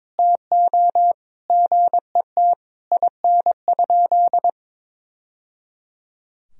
Morse code, heard as a keyed tone, about 22 wpm, 700 Hz. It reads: TO GET IN?
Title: Sentences from Top 1000 Words - Review Part 4 - 22wpm